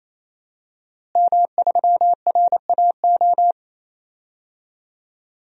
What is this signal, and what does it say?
Morse code, heard as a keyed tone, about 28 wpm, 700 Hz. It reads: M3RAO
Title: DXCC-100 Call Signs - 5 Characters - Repeated 3x - 28wpm